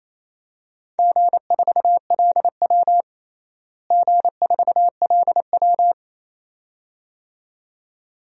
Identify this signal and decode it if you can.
Morse code, heard as a keyed tone, about 28 wpm, 700 Hz. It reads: G4LW G4LW
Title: DXCC-100 Call Signs - 4 Characters - Repeated 2x - 28wpm